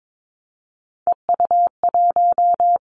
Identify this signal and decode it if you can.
Morse code, heard as a keyed tone, about 22 wpm, 700 Hz. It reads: EU1
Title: Callsign Prefixes for Top 50 DXCC - 22wpm